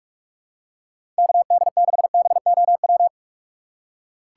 Morse code, heard as a keyed tone, about 46 wpm, 700 Hz. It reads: KD6BOW
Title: USA Call Signs - 4x Character Spacing - 45wpm